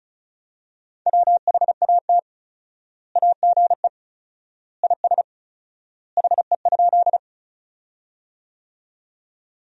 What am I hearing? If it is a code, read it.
Morse code, heard as a keyed tone, about 34 wpm, 700 Hz. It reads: WHAT AGE IS HE?
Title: 4x Word-Spacing - Sentences from Top 500 Words - 35wpm